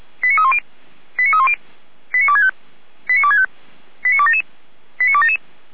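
Five-tone selective call (ZVEI), heard as quick runs of five tones, tones of about 70 ms. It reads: 89310 89310 89265 89265 89277 89277
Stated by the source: ZVEI five-tone selective call 89108 received on SDR, feuerwehr-orlamuende (fire brigade Orlamünde) archive.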